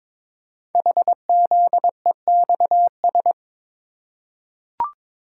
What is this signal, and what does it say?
Morse code, heard as a keyed tone, about 22 wpm, 700 Hz. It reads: HZEXS
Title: Code Groups - 5 Letters - 22wpm